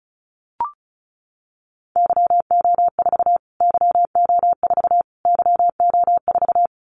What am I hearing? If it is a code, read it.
Morse code, heard as a keyed tone, about 34 wpm, 700 Hz. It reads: YO4 YO4 YO4
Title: Callsign Prefixes for Top 25 DXCC - Repeated 3x - 35wpm